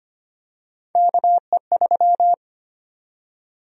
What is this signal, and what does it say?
Morse code, heard as a keyed tone, about 25 wpm, 700 Hz. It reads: KE3
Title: USA Call Sign Prefixes - 25wpm